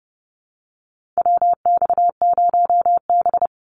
Morse code, heard as a keyed tone, about 30 wpm, 700 Hz. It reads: WX0B
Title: Call Signs Repeated 2x - 30wpm